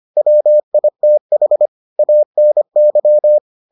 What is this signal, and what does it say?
Morse code, heard as a keyed tone, about 25 wpm, 600 Hz. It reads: WITH ANY